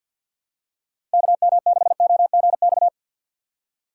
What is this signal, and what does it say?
Morse code, heard as a keyed tone, about 47 wpm, 700 Hz. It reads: KM6OGX
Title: USA Call Signs - 6 Characters - 6x Character Spacing - 50wpm